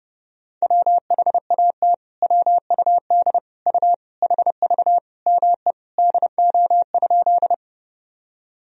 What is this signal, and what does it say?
Morse code, heard as a keyed tone, about 30 wpm, 700 Hz. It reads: WHAT WUD U HV ME DO?
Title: Sentences from Top 200 Words and Abbreviations 30wpm